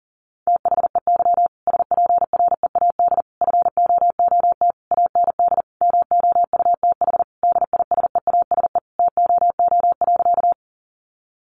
Morse code, heard as a keyed tone, about 40 wpm, 700 Hz. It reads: THEY SPREAD FOOT AND MOUTH DISEASE TOO.
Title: Sentences from Top 1500 Words 40wpm